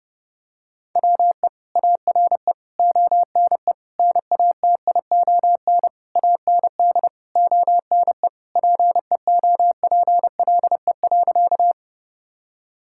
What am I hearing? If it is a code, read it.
Morse code, heard as a keyed tone, about 30 wpm, 700 Hz. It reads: WE ARE ONE NATION AND ONE PEOPLE.
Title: Sentences from Top 1000 Words - Review Part 6 - 30wpm